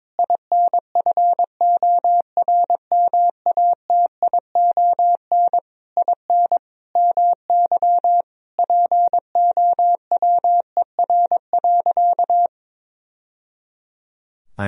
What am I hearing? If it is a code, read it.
Morse code, heard as a keyed tone, about 22 wpm, 700 Hz. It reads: INFORMATION IN MY POWER.
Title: Sentences from Top 1000 Words - Review Part 3 - 22wpm